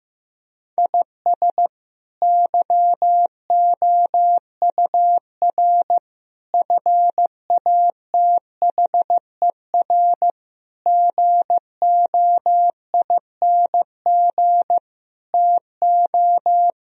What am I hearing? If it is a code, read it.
Morse code, heard as a keyed tone, about 15 wpm, 700 Hz. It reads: IS YOUR FATHER GOING TO
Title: Sentences from Top 1000 Words - Review Part 2 - 15wpm